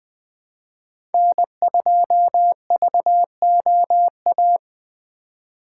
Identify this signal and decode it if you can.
Morse code, heard as a keyed tone, about 20 wpm, 700 Hz. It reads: N2VOA
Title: Speed Racing - USA 1x3 Call Signs 17wpm